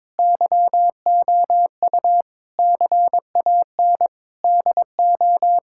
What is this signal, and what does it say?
Morse code, heard as a keyed tone, about 22 wpm, 700 Hz. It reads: YOU CAN DO
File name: Sentences with 7-Word Phrases - 22wpm